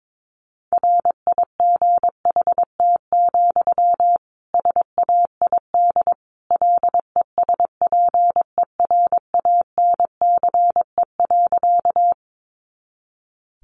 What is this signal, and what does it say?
Morse code, heard as a keyed tone, about 22 wpm, 700 Hz. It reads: RIGHT, SAID LESPERANCE.
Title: A Sound Of Thunder 22wpm